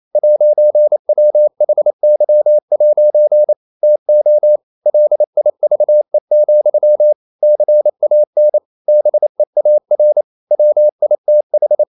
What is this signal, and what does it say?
Morse code, heard as a keyed tone, about 28 wpm, 600 Hz. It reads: 'WHY' TO LIVE, CAN BEAR WITH
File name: CW / Practice / EnglishQuotations0228